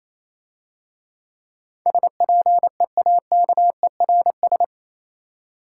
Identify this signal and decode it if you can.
Morse code, heard as a keyed tone, about 27 wpm, 700 Hz. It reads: SPEAKERS
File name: Easy Words - Speed-Racing with Hints - 22wpm